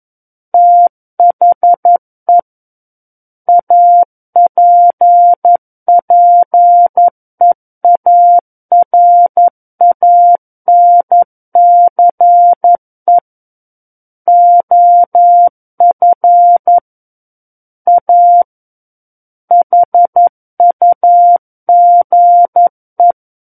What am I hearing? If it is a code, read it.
Morse code, heard as a keyed tone, about 11 wpm, 700 Hz. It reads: THE APPEARANCE OF A HUGE
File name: 11 WPM